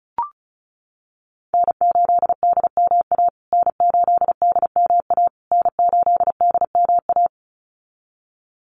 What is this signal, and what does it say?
Morse code, heard as a keyed tone, about 34 wpm, 700 Hz. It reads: N8DMA N8DMA N8DMA
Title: USA Call Signs Repeated 3x 35wpm